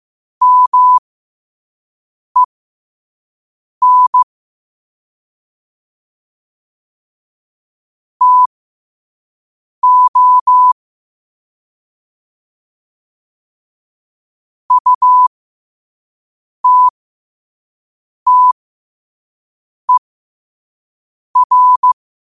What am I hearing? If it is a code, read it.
Morse code, heard as a keyed tone, about 15 wpm, spaced as for 5 wpm, 1000 Hz. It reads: MEN TO UTTER
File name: qotdfiles / 5wpm